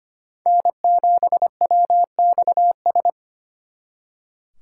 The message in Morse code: N7WXS

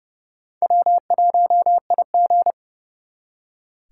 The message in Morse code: W1IG

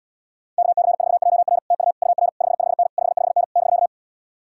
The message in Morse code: KC4JR TU UR 55N 55N <BK>

